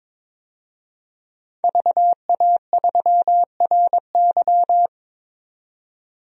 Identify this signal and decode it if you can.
Morse code, heard as a keyed tone, about 22 wpm, 700 Hz. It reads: VA3RY
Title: DXCC-100 Call Signs - 5 Characters - 22wpm